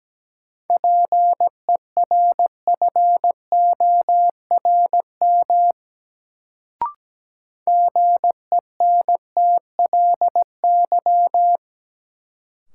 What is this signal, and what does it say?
Morse code, heard as a keyed tone, about 17 wpm, 700 Hz. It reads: PERFORM GENTLY
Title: Sentences from Top 1500 Words - New Word Drill Alt - 17wpm